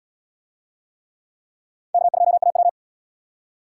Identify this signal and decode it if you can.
Morse code, heard as a keyed tone, about 78 wpm, 700 Hz. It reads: K2AW